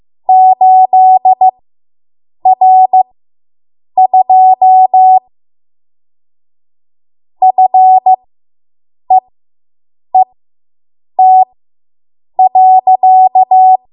Morse code, heard as a keyed tone, about 15 wpm, spaced as for 7 wpm, 750 Hz. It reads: 8R2 FEET.